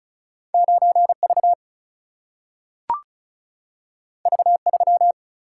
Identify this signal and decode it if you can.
Morse code, heard as a keyed tone, about 36 wpm, 700 Hz. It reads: 9V V3
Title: Single-Letter Single-Number 35wpm